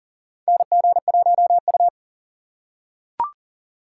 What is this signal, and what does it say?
Morse code, heard as a keyed tone, about 40 wpm, 700 Hz. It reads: NG1U
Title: USA Call Signs - 4 Characters - 3x Character Spacing - 40wpm